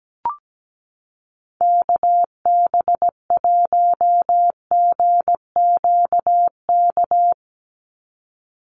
Morse code, heard as a keyed tone, about 17 wpm, 700 Hz. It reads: KB1GQK
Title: USA Call Signs 17wpm